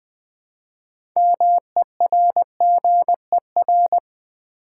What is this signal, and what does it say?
Morse code, heard as a keyed tone, about 20 wpm, 700 Hz. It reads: MERGER
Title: 6 Letter Words - First 3 Letters Given - 20wpm